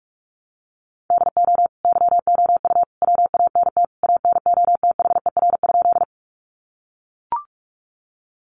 Morse code, heard as a keyed tone, about 44 wpm, 700 Hz. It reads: DO YOU WANT ANOTHER?
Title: Sentences from Top 600 Words 45wpm